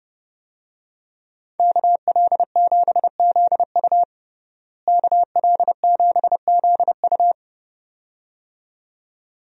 Morse code, heard as a keyed tone, about 32 wpm, 700 Hz. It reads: KL7ZU KL7ZU
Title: USA Call Signs - 5 Characters - Repeated 2x - 30wpm